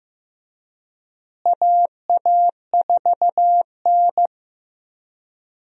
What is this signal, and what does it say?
Morse code, heard as a keyed tone, about 15 wpm, 700 Hz. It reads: AA4N
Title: USA Call Signs - 4 Characters - Repeated 3x - 15wpm